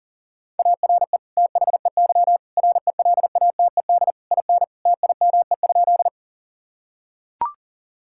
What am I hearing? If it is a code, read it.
Morse code, heard as a keyed tone, about 40 wpm, 700 Hz. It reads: ARE THEY RELATED IN TIME?
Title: Sentences from Top 1000 Words - Review Part 5 - 40wpm